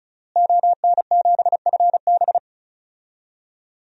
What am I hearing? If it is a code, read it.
Morse code, heard as a keyed tone, about 36 wpm, 700 Hz. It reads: ON7FB
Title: DXCC-100 Call Signs - 5 Characters - Repeated 3x - 35wpm